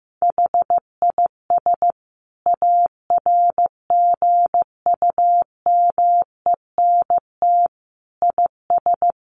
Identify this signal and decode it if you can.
Morse code, heard as a keyed tone, about 15 wpm, 700 Hz. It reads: HIS ARGUMENT IS
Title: Sentences from Top 1400 Words 15wpm